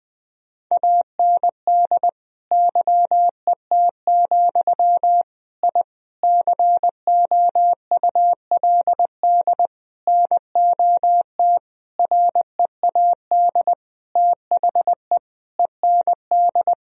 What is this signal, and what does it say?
Morse code, heard as a keyed tone, about 21 wpm, 700 Hz. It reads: AND YET, I COULD NOT READ THE END
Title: Sentences from Top 1000 Words - Review Part 1 - 20wpm